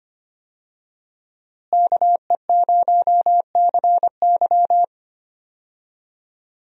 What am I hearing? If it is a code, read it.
Morse code, heard as a keyed tone, about 25 wpm, 700 Hz. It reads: KE0CY
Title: USA Call Signs - 5 Characters - Repeated 2x - 25wpm